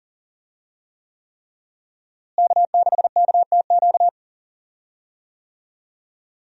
Morse code, heard as a keyed tone, about 40 wpm, 700 Hz. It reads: K6KTQ
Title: USA Call Signs - 5 Characters - Repeated 3x - 40wpm